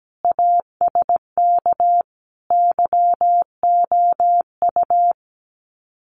Morse code, heard as a keyed tone, about 17 wpm, 700 Hz. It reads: ASK YOU
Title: Sets of 5 Words Encore2 17wpm